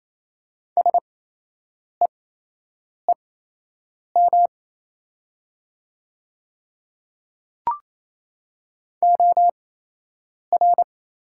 Morse code, heard as a keyed tone, about 28 wpm, spaced as for 8 wpm, 690 Hz. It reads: SEEM OR
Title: 8x Character Spacing - Top 300 Words - 28wpm